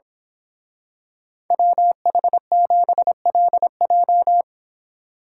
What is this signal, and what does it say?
Morse code, heard as a keyed tone, about 25 wpm, 700 Hz. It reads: WH7LJ
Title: Speed Racing - USA 2x2 Call Signs 22wpm